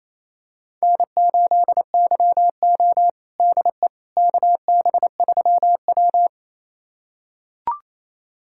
Morse code, heard as a keyed tone, about 28 wpm, 700 Hz. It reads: N8YO DE KB3W
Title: USA Callsign DE Callsign - 4 Characters - 28wpm